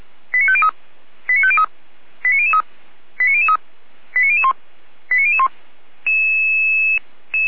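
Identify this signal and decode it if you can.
Five-tone selective call (ZVEI), heard as quick runs of five tones, tones of about 70 ms. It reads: 89503 89503 89003 89003 89001 89001
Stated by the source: ZVEI five-tone call received by SDR, from audio.f85360.feuerwehr-orlamuende.org, a fire brigade (Feuerwehr Orlamünde) server.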